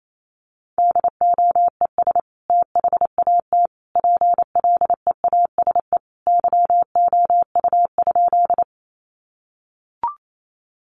Morse code, heard as a keyed tone, about 28 wpm, 700 Hz. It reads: DOES THAT PLEASE YOU?
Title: Sentences from Top 1500 Words 28wpm